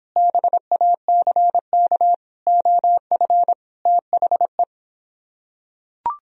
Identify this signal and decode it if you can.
Morse code, heard as a keyed tone, about 26 wpm, 700 Hz. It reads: BACK OF THE